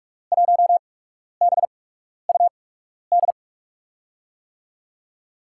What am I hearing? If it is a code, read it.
Morse code, heard as a keyed tone, about 44 wpm, spaced as for 12 wpm, 700 Hz. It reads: 1BUD
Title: USA Call Signs - 6 Characters - 8x Character Spacing - 45wpm